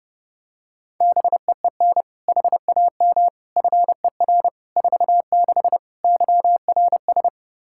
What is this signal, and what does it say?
Morse code, heard as a keyed tone, about 32 wpm, 700 Hz. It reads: BEEN HAM FER 46 YRS
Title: Practice QSO - Exchange 3 - More Info - 30wpm